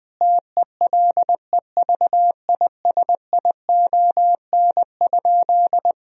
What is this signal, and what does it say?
Morse code, heard as a keyed tone, about 20 wpm, 700 Hz. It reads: TELEVISION?
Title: Sentences from Top 1000 Words - Review Part 4 - 20wpm